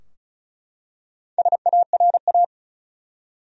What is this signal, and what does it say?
Morse code, heard as a keyed tone, about 35 wpm, 700 Hz. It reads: SARA
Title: USA Common Names - Max 5 Letters - 35wpm